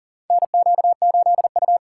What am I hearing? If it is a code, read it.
Morse code, heard as a keyed tone, about 40 wpm, 700 Hz. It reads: NQ8U
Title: USA Call Signs - 4x Character Spacing - 40wpm